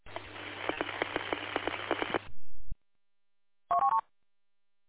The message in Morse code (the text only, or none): none